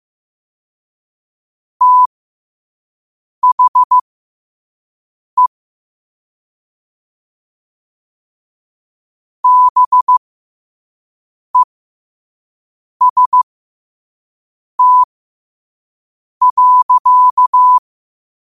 Morse code, heard as a keyed tone, about 15 wpm, spaced as for 5 wpm, 1000 Hz. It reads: THE BEST.
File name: qotdfiles / 5wpm